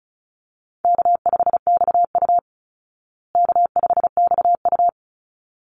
Morse code, heard as a keyed tone, about 34 wpm, 700 Hz. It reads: K5XU K5XU
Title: Easy Call Signs Repeated 2x - 35wpm